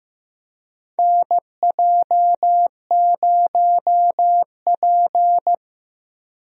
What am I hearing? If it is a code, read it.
Morse code, heard as a keyed tone, about 15 wpm, 700 Hz. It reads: NJ0P
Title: USA Call Signs - 4 Characters - Repeated 3x - 15wpm